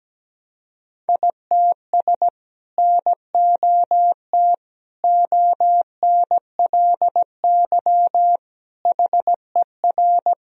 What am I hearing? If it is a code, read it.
Morse code, heard as a keyed tone, about 17 wpm, 700 Hz. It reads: ITS NOT ONLY HER
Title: Sentences from Top 1000 Words - Review Part 2 - 17wpm